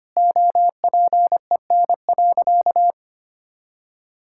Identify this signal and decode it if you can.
Morse code, heard as a keyed tone, about 25 wpm, 700 Hz. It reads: OPEN.
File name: practice / 25wpm / All-Age Appropriate Jokes - 25wpm